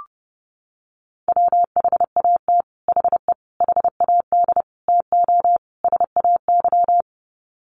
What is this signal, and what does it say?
Morse code, heard as a keyed tone, about 30 wpm, 700 Hz. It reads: WHAT HE HAD TO SAY
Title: Sets of 5 Words Encore2 30wpm